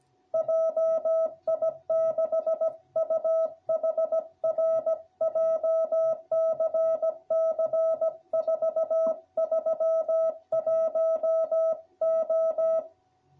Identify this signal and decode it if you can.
Morse code, heard as a keyed tone, about 16 wpm, 660 Hz. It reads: JI6UHR JCC 431O